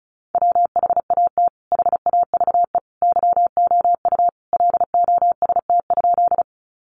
Morse code, heard as a keyed tone, about 34 wpm, 700 Hz. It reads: WHAT HAVE YOU LOST?
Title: Sentences from Top 1400 Words 35wpm